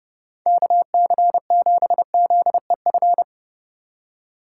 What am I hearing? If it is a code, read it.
Morse code, heard as a keyed tone, about 30 wpm, 700 Hz. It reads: KC7ZEF
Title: USA Call Signs - 6 Characters - 2x Character Spacing - 30wpm